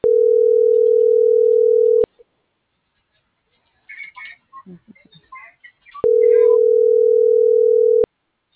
Ringback tone (North America), 440 + 480 Hz.